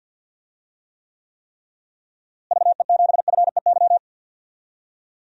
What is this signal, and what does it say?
Morse code, heard as a keyed tone, about 49 wpm, 700 Hz. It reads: VE7FEY
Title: DXCC-100 Call Signs - 6 Characters - Repeated 3x - 50wpm